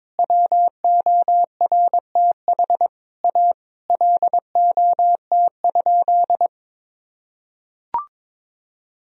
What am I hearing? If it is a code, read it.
Morse code, heard as a keyed tone, about 21 wpm, 700 Hz. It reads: WORTH A LOT?
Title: Sentences from Top 1000 Words - Review Part 4 - 22wpm